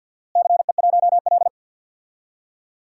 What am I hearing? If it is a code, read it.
Morse code, heard as a keyed tone, about 50 wpm, 700 Hz. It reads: CE1L
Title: DXCC-100 Call Signs - 4 Characters - Repeated 3x - 50wpm